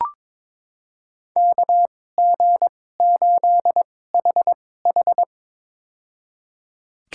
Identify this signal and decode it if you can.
Morse code, heard as a keyed tone, about 22 wpm, spaced as for 16 wpm, 700 Hz. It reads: KG8HH
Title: USA Call Signs - 2x Character Spacing - 22wpm